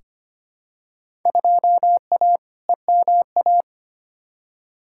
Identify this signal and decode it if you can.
Morse code, heard as a keyed tone, about 25 wpm, 700 Hz. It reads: 2A EMA
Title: Field Day Exchanges Encore - 25wpm